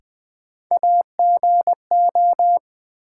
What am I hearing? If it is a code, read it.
Morse code, heard as a keyed tone, about 20 wpm, 700 Hz. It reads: AGO
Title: Sets of 2 Words - New Word Drill - 20wpm